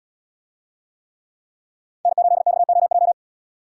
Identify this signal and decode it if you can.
Morse code, heard as a keyed tone, about 77 wpm, 700 Hz. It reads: N8XOJ